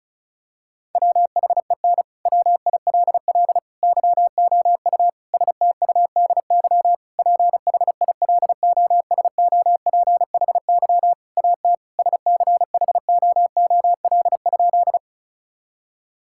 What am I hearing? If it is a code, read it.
Morse code, heard as a keyed tone, about 36 wpm, 700 Hz. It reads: WHEN WILL YOU STUDY PHILOSOPHY AT SCHOOL?
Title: Sentences from Top 1000 Words - Review Part 6 - 35wpm